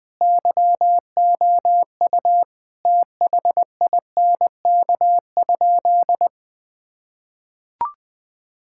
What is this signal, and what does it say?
Morse code, heard as a keyed tone, about 20 wpm, 700 Hz. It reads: YOU THINK?
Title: Sentences with 3-Word Phrases - 20wpm